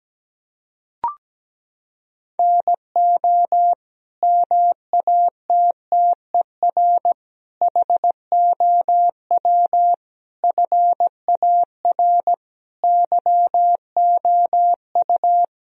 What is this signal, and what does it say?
Morse code, heard as a keyed tone, about 17 wpm, 700 Hz. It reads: NO MATTER HOW FAR YOU
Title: Sentences from Top 1000 Words - Review Part 1 - 17wpm